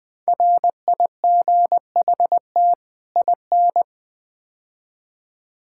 Morse code, heard as a keyed tone, about 21 wpm, 700 Hz. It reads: RIGHT IN
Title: Build A Phrase - 5-Word Phrases - 20wpm